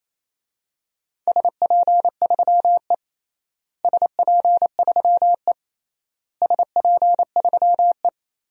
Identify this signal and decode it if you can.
Morse code, heard as a keyed tone, about 28 wpm, 700 Hz. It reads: SP3E SP3E SP3E
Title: DXCC-100 Call Signs - 4 Characters - Repeated 3x - 28wpm